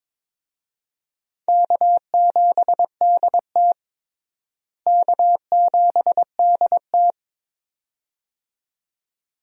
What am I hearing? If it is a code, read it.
Morse code, heard as a keyed tone, about 22 wpm, 700 Hz. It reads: K7DT K7DT
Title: USA Call Signs - 4 Characters - Repeated 2x - 22wpm